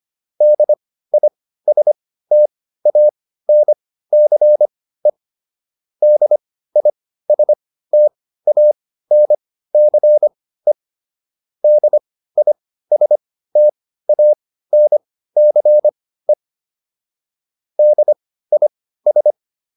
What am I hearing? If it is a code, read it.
Morse code, heard as a keyed tone, about 25 wpm, spaced as for 15 wpm, 600 Hz. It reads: DISTANCE DISTANCE DISTANCE DIS